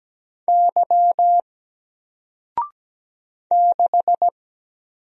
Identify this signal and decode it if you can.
Morse code, heard as a keyed tone, about 17 wpm, 700 Hz. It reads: Y 6
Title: Single Letter-Number Alt 17wpm